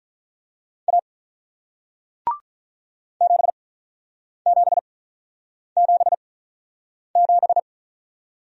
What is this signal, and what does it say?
Morse code, heard as a keyed tone, about 48 wpm, spaced as for 9 wpm, 700 Hz. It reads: A 7777